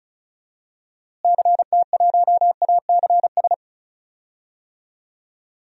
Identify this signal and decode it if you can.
Morse code, heard as a keyed tone, about 36 wpm, 700 Hz. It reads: CT1ACS
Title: DXCC-100 Call Signs - 6 Characters - Repeated 2x - 35wpm